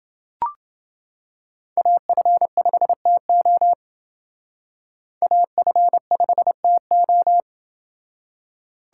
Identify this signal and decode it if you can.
Morse code, heard as a keyed tone, about 28 wpm, 700 Hz. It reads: AF5TO AF5TO